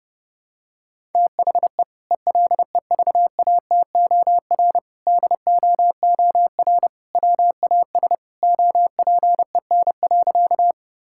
Morse code, heard as a keyed tone, about 30 wpm, 700 Hz. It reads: THE ELEVATOR DOOR WAS OPEN.
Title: Sentences from Top 1600 Words 30wpm